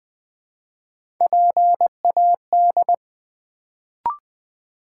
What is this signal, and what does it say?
Morse code, heard as a keyed tone, about 20 wpm, 700 Hz. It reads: PAD
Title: Build A Word - 1 Letter At A Time - 3-Letter Words - 20wpm